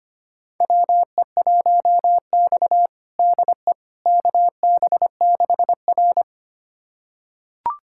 Morse code, heard as a keyed tone, about 25 wpm, 700 Hz. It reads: WE1X DE KB6R
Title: USA Callsign DE Callsign - 4 Characters - 25wpm